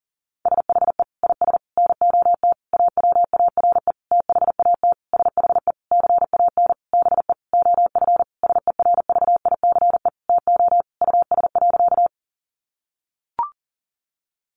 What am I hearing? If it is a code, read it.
Morse code, heard as a keyed tone, about 40 wpm, 700 Hz. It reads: SHE IS NOT AWARE THAT SHE CAN BE OF SERVICE TO US.
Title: Sentences from Top 1200 Words 40wpm